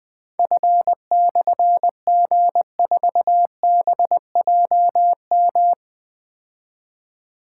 Morse code, heard as a keyed tone, about 21 wpm, 700 Hz. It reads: F/G4BJM
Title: Difficult Call Signs 20wpm